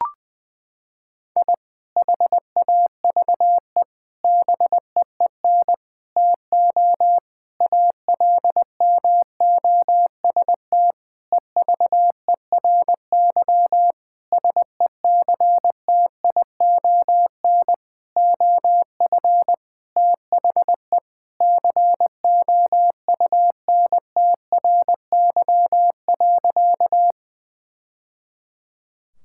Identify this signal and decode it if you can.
Morse code, heard as a keyed tone, about 20 wpm, 700 Hz. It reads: I HAVE BEEN TO ALMOST EVERY SECTION OF THE COUNTRY.
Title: Sentences from Top 1000 Words - Review Part 5 - 20wpm